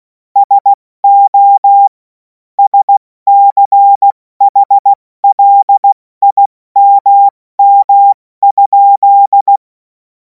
Morse code, heard as a keyed tone, about 16 wpm, 800 Hz. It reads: SO SCHLIMM?
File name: Morsekurs / Witze / Witz 07. Juli 2026 70 F80